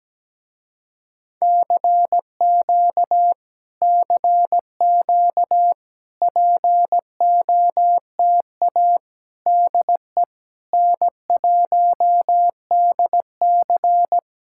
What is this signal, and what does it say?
Morse code, heard as a keyed tone, about 17 wpm, 700 Hz. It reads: CQ CQ POTA DE N1DC
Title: Speed-Racing Common QSO Phrases Encore - 17wpm